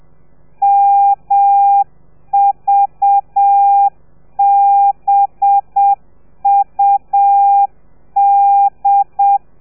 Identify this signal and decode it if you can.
Morse code, heard as a keyed tone, about 7 wpm, 800 Hz. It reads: MVBUD
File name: telegrafia / i2tfj / Lez4 30